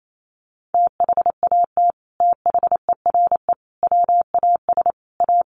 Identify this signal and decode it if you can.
Morse code, heard as a keyed tone, about 28 wpm, 700 Hz. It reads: THAT THERE WAS A
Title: Sets of 4 Words 28wpm